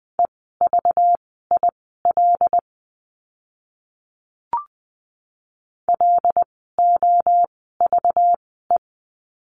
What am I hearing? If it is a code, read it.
Morse code, heard as a keyed tone, about 20 wpm, spaced as for 15 wpm, 700 Hz. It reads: EVIL LOVE
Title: 192 Lesson 31f - Letter v - Words 20wpm